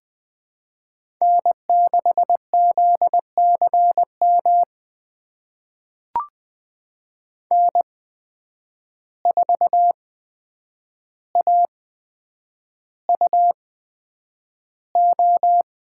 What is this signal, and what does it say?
Morse code, heard as a keyed tone, about 20 wpm, spaced as for 9 wpm, 700 Hz. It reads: N6ZCM N4AUO
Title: USA Call Signs - 5 Characters - 8x Character Spacing - 20wpm